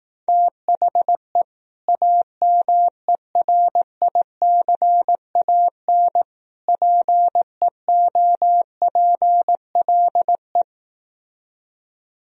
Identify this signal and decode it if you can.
Morse code, heard as a keyed tone, about 18 wpm, 700 Hz. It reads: THE AMERICAN PEOPLE
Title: Speed-Racing - Sets of 3 Words 15wpm